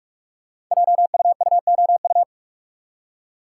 Morse code, heard as a keyed tone, about 45 wpm, 700 Hz. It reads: JUUOU
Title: Code Groups - 5 Letters - 45wpm